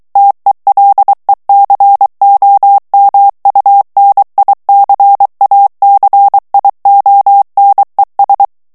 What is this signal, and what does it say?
Morse code, heard as a keyed tone, about 23 wpm, 800 Hz. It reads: TELECOMUNICACIONES
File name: morse mp3 / ESPAGNOL TEXTE EN CLAIR 2 VITESSE 22 MOTS MINUTES